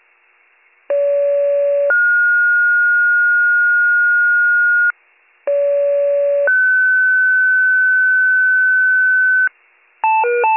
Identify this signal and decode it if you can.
Two-tone page, heard as a one-second tone, then a three-second tone, about 570 then 1430 Hz, about 570 then 1500 Hz. It is heard from 0.9 to 9.5 s.